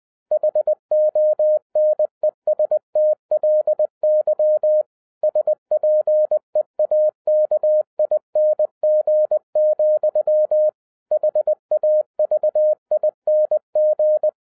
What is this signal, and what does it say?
Morse code, heard as a keyed tone, about 20 wpm, 610 Hz. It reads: HONESTLY SPEAKING, HAVING
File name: 2012-01-mp3 / n16-THROUGH EIGHT MONTHS-20wpm